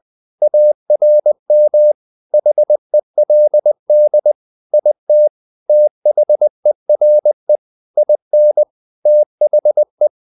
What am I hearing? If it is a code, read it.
Morse code, heard as a keyed tone, about 20 wpm, 600 Hz. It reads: ARM HELD IT THERE IN THE
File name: the old man and the sea / the old man and the sea part 1 20wpm